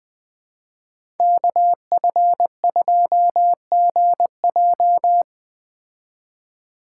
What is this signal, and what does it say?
Morse code, heard as a keyed tone, about 20 wpm, 700 Hz. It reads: KF2GJ